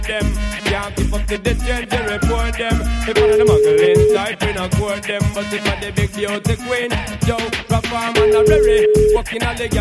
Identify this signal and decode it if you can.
Ringback tone (Europe, 425 Hz), with noise.